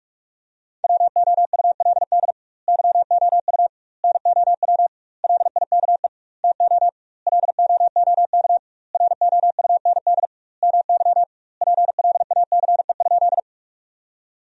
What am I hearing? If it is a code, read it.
Morse code, heard as a keyed tone, about 46 wpm, 700 Hz. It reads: WOULD YOU NOW LIKE TO LOOK ROUND MY PLACE?
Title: 2x Word-Spacing - Sentences from Top 200 Words - 45wpm